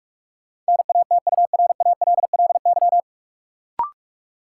Morse code, heard as a keyed tone, about 46 wpm, 700 Hz. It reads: NATURALLY
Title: Sentences from Top 1300 Words - New Word Drill - 45wpm